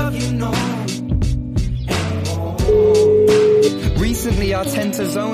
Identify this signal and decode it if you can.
Ringback tone (Europe, 425 Hz), with speech.